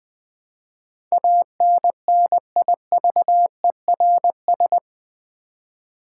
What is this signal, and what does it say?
Morse code, heard as a keyed tone, about 20 wpm, 700 Hz. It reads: ANNIVERS
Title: Build A Word - 4 Letters At A Time - 11-Letter Words - 20wpm